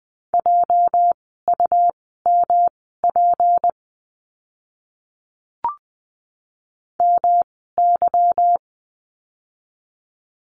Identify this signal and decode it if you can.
Morse code, heard as a keyed tone, about 19 wpm, spaced as for 14 wpm, 700 Hz. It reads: JUMP MY
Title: 206 Lesson 33f - Letter j - Words 20wpm